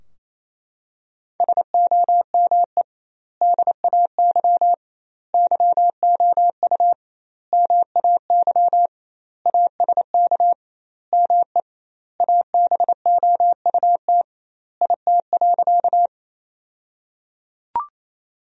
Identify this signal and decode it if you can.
Morse code, heard as a keyed tone, about 28 wpm, 700 Hz. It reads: SOME DAY YOU MAY ASK ME ABOUT IT.
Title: 2x Word-Spacing - Sentences from Top 200 Words - 28wpm